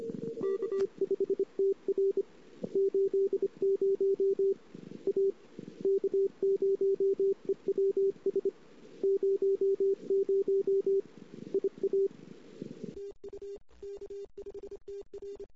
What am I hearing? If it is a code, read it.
Morse code, heard as a keyed tone, about 28 wpm, 390 Hz. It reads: K5TR 80 A K0EWS 00 IA TU K5TR
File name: CW / 2008-ss-cw-k5tr-op-n5rz -14:00